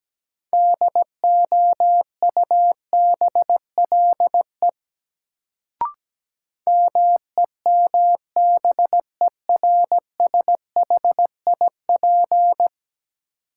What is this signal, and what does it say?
Morse code, heard as a keyed tone, about 17 wpm, 700 Hz. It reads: DOUBLE MEMBERSHIP
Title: Sentences from Top 1400 Words - New Word Drill - 17wpm